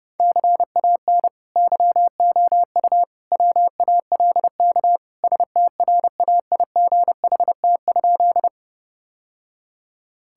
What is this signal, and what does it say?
Morse code, heard as a keyed tone, about 30 wpm, 700 Hz. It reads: CAN YOU WALK STRAIGHT?
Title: Sentences from Top 1000 Words - Review Part 6 - 30wpm